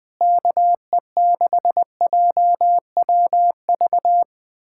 Morse code, heard as a keyed tone, about 20 wpm, 700 Hz. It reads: KE6JWV
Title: USA Callsign DE Callsign 20wpm